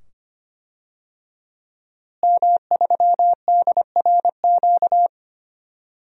Morse code, heard as a keyed tone, about 25 wpm, 700 Hz. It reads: M3DRQ
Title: DXCC-100 Call Signs - 5 Characters - Repeated 2x - 25wpm